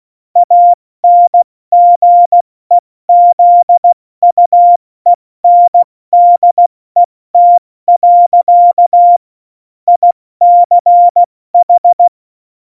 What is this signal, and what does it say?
Morse code, heard as a keyed tone, about 16 wpm, 700 Hz. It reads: ANGEZUENDET. ICH